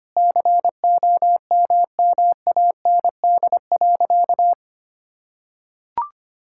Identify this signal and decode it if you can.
Morse code, heard as a keyed tone, about 25 wpm, 700 Hz. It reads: COMMAND.
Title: Sentences from Top 1600 Words 25wpm